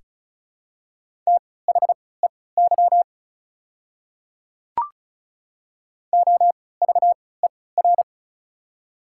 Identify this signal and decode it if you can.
Morse code, heard as a keyed tone, about 33 wpm, spaced as for 20 wpm, 690 Hz. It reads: THEY OVER
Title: 3x Character Spacing - Top 100 Words - 35wpm